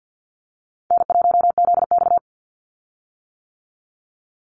Morse code, heard as a keyed tone, about 50 wpm, 700 Hz. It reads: N1ZX
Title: Easy Call Signs - 50wpm